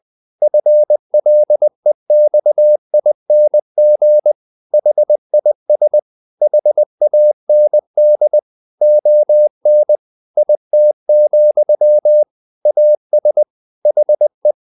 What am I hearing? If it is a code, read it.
Morse code, heard as a keyed tone, about 20 wpm, 600 Hz. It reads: FLEXING HIS HAND ON IT, AS HE
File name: the old man and the sea part 2 20wpm